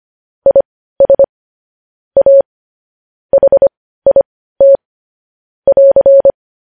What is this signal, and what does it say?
Morse code, heard as a keyed tone, about 25 wpm, spaced as for 15 wpm, 550 Hz. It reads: IS A HIT <AR>
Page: news headlines in morse code at 15 wpm